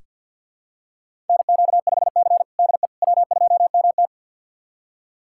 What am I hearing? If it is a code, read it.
Morse code, heard as a keyed tone, about 47 wpm, 700 Hz. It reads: NQ5C DE W2MT
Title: USA Callsign DE Callsign - 4 Characters - 50wpm